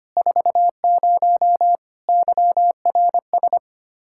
Morse code, heard as a keyed tone, about 25 wpm, 700 Hz. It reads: 40 YRS
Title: Practice QSO - Exchange 3 - More Info - 25wpm